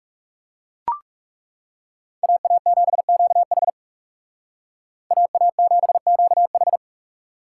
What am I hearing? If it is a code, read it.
Morse code, heard as a keyed tone, about 38 wpm, 700 Hz. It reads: AA7QH AA7QH